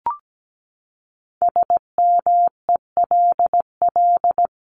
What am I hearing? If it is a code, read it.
Morse code, heard as a keyed tone, about 17 wpm, 700 Hz. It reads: SMELL
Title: Top 1501-1750 Words 17wpm